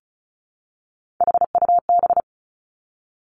Morse code, heard as a keyed tone, about 36 wpm, 700 Hz. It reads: HUB
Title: Three Challenging Letter-Number Combinations 35wpm